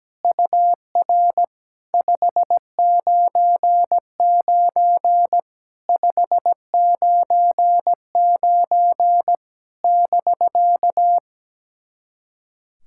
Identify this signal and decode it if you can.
Morse code, heard as a keyed tone, about 17 wpm, 700 Hz. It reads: UR 599 599 <BK>
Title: SOTA QSO Practice 17wpm